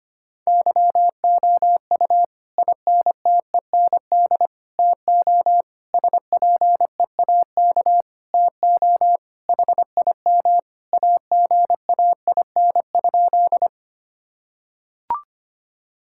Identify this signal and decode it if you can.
Morse code, heard as a keyed tone, about 25 wpm, 700 Hz. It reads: YOU INTEND TO SPEAK TO HIM AGAIN?
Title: Sentences from Top 1600 Words 25wpm